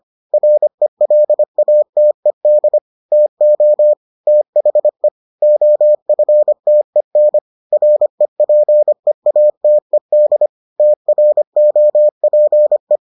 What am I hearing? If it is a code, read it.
Morse code, heard as a keyed tone, about 25 wpm, 600 Hz. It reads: RELATED TO THE OFTEN REPEATED TROPE